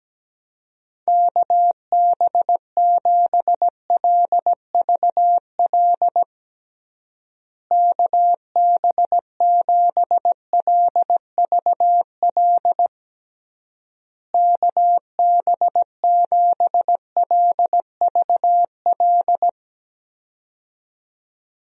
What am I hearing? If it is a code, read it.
Morse code, heard as a keyed tone, about 17 wpm, 700 Hz. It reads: KB7LVL KB7LVL KB7LVL
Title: USA Call Signs - 6 Characters - Repeated 3x - 17wpm